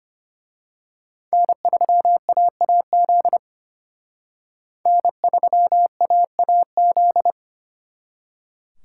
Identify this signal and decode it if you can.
Morse code, heard as a keyed tone, about 29 wpm, 700 Hz. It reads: N3AAZ N3AAZ
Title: Speed Racing - USA 1x3 Call Signs 25wpm